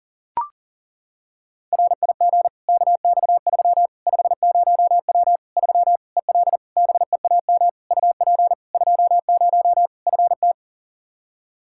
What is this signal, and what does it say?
Morse code, heard as a keyed tone, about 40 wpm, 700 Hz. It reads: RIG KX3 50W 3 EL BEAM UP 20 FT
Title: Practice QSO - Exchange 2 - Info - 40wpm